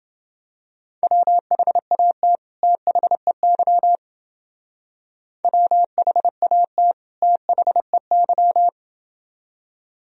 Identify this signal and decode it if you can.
Morse code, heard as a keyed tone, about 29 wpm, 700 Hz. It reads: WHAT THEY WHAT THEY